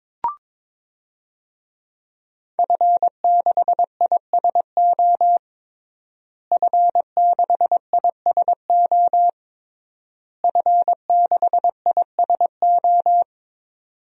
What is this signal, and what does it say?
Morse code, heard as a keyed tone, about 22 wpm, 700 Hz. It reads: F6ISO F6ISO F6ISO
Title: DXCC-100 Call Signs - 5 Characters - Repeated 3x - 22wpm